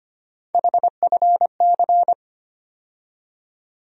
Morse code, heard as a keyed tone, about 25 wpm, 700 Hz. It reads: HFC
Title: USA Call Sign Postfixes - 3 Letters 25wpm